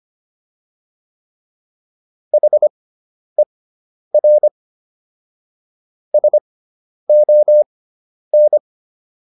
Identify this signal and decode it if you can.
Morse code, heard as a keyed tone, about 25 wpm, spaced as for 10 wpm, 600 Hz. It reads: HER SON